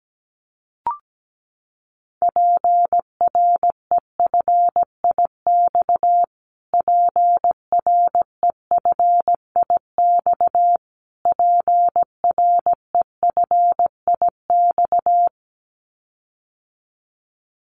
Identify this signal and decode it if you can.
Morse code, heard as a keyed tone, about 17 wpm, 700 Hz. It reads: PREFIX PREFIX PREFIX
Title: Callsign Prefixes for Top 25 DXCC - Repeated 3x - 17wpm